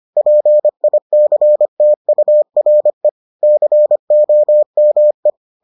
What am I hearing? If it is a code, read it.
Morse code, heard as a keyed tone, about 25 wpm, 600 Hz. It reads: PICTURE COME